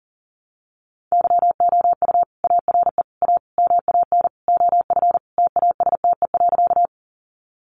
Morse code, heard as a keyed tone, about 40 wpm, 700 Hz. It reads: YOU ARE A MAN OF TASTE.
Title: Sentences from Top 1100 Words 40wpm